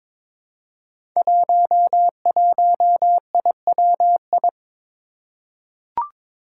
Morse code, heard as a keyed tone, about 22 wpm, 700 Hz. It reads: 11IWI
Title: Five Challenging Letter-Number Combinations 22wpm